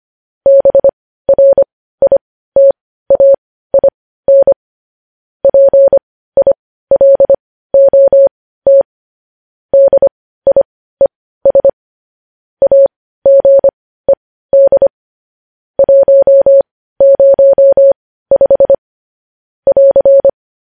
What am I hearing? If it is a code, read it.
Morse code, heard as a keyed tone, about 25 wpm, spaced as for 15 wpm, 550 Hz. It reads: BRITAIN PILOT DIES AGED 105 <AR>
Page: news-headlines-in-morse-code-at-15-wpm